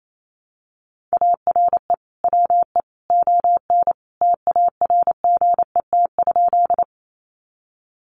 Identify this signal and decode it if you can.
Morse code, heard as a keyed tone, about 28 wpm, 700 Hz. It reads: ARE WE ON TARGET?
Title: Sentences from Top 1200 Words 28wpm